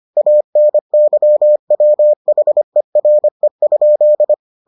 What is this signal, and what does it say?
Morse code, heard as a keyed tone, about 25 wpm, 600 Hz. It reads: ANYWHERE?